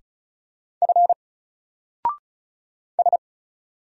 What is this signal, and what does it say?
Morse code, heard as a keyed tone, about 33 wpm, 700 Hz. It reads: F S
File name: Single-Letter-Number-35wpm